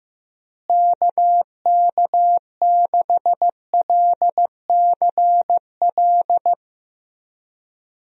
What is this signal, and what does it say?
Morse code, heard as a keyed tone, about 15 wpm, 700 Hz. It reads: KK6LCL
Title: USA Call Signs - 6 Characters - Repeated 2x - 15wpm